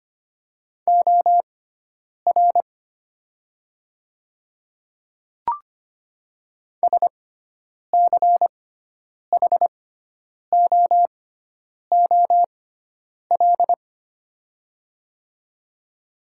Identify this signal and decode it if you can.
Morse code, heard as a keyed tone, about 25 wpm, spaced as for 9 wpm, 700 Hz. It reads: OR SCHOOL
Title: 6x Character Spacing - Top 500 Words - 25wpm